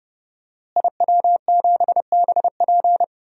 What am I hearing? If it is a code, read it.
Morse code, heard as a keyed tone, about 32 wpm, 700 Hz. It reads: IW7BP
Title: DXCC-100 Call Signs - 5 Characters - Repeated 3x - 30wpm